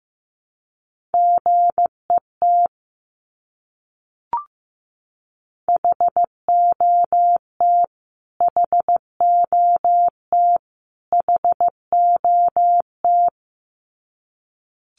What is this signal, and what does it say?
Morse code, heard as a keyed tone, about 14 wpm, 700 Hz. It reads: GET HOT HOT HOT
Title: Top 200 Words Repeated 15wpm